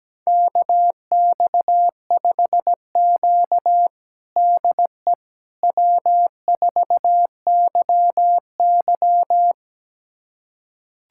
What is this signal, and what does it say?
Morse code, heard as a keyed tone, about 17 wpm, 700 Hz. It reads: KX5Q DE W4YY
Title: USA Callsign DE Callsign - 4 Characters - 17wpm